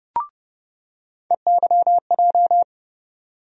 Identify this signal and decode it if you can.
Morse code, heard as a keyed tone, about 30 wpm, 700 Hz. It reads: EYJ
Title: USA Call Sign Postfixes - 3 Letters 30wpm